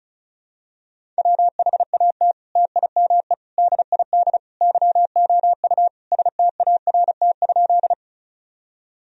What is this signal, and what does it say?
Morse code, heard as a keyed tone, about 36 wpm, 700 Hz. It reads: WHAT TIME DID YOU START?
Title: Sentences from Top 1000 Words - Review Part 1 - 35wpm